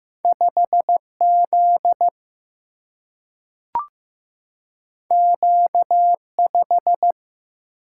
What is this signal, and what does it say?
Morse code, heard as a keyed tone, about 15 wpm, 700 Hz. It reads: 5Z Q5
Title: Single-Letter Single-Number 15wpm